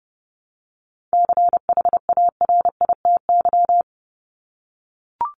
Morse code, heard as a keyed tone, about 30 wpm, 700 Hz. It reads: CHARITY
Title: Top 1501-1750 Words 30wpm